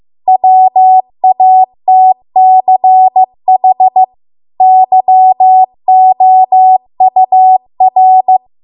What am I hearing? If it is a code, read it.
Morse code, heard as a keyed tone, about 15 wpm, 750 Hz. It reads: WATCH YOUR